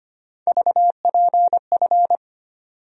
Morse code, heard as a keyed tone, about 25 wpm, 700 Hz. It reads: VPF
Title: USA Call Sign Postfixes - 3 Letters 25wpm